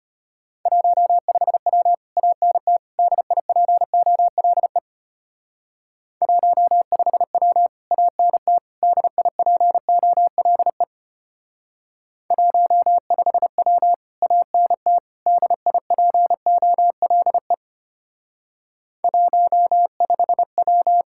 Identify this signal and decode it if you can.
Morse code, heard as a keyed tone, about 32 wpm, 700 Hz. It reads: 15W ANT DIPOLE 15W ANT DIPOLE 15W ANT DIPOLE 15W